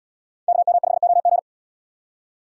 Morse code, heard as a keyed tone, about 73 wpm, 690 Hz. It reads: KM5OG